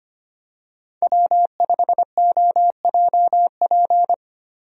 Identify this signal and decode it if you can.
Morse code, heard as a keyed tone, about 25 wpm, 700 Hz. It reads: W5OJP
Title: USA Call Signs - 5 Characters - 25wpm